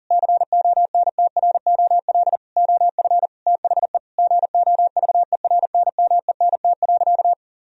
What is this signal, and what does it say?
Morse code, heard as a keyed tone, about 40 wpm, 700 Hz. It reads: CONTROL OF THE GOVERNMENT.
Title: Sentences from Top 1000 Words - Review Part 3 - 40wpm